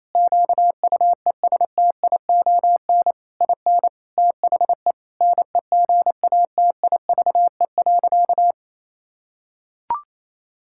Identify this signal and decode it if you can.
Morse code, heard as a keyed tone, about 28 wpm, 700 Hz. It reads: QUESTION IN THE NEGATIVE.
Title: Sentences from Top 1000 Words - Review Part 6 - 28wpm